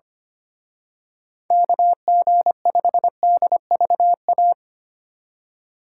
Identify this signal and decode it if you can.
Morse code, heard as a keyed tone, about 28 wpm, 700 Hz. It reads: KG5DVA
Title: Speed Racing - USA 2x3 Call Signs 25wpm